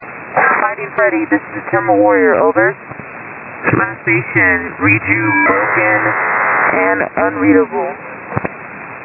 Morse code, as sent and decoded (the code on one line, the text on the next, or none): none